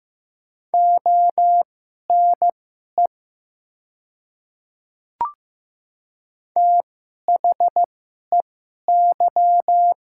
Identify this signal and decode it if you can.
Morse code, heard as a keyed tone, about 15 wpm, spaced as for 11 wpm, 700 Hz. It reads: ONE THEY
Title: 2x Character Spacing - Top 100 Words - 15wpm